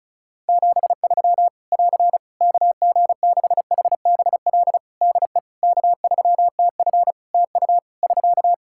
Morse code, heard as a keyed tone, about 36 wpm, 700 Hz. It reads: 73 <AR> KG6HBL DE K3TF TU <SK>
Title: Practice QSO - Exchange 4 - Ending - 35wpm